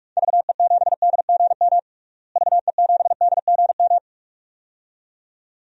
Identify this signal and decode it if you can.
Morse code, heard as a keyed tone, about 41 wpm, 700 Hz. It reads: VE7DGM VE7DGM
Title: DXCC-100 Call Signs - 6 Characters - Repeated 2x - 45wpm